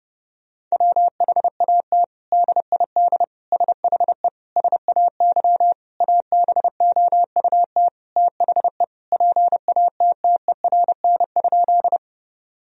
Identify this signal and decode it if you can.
Morse code, heard as a keyed tone, about 30 wpm, 700 Hz. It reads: WHAT DID SHE SAY ABOUT THE PATTERN?
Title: Sentences from Top 1000 Words - Review Part 2 - 30wpm